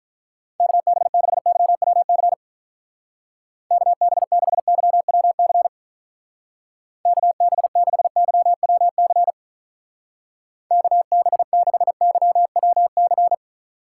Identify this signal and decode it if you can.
Morse code, heard as a keyed tone, about 42 wpm, 700 Hz. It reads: KB6YWC KB6YWC KB6YWC KB6YWC
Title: Speed Racing - USA 2x3 Call Signs 35wpm